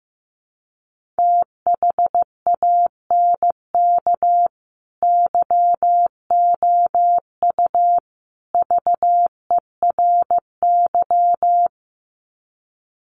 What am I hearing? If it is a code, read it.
Morse code, heard as a keyed tone, about 15 wpm, 700 Hz. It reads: THANK YOU VERY
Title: Sets of 3 Words 15wpm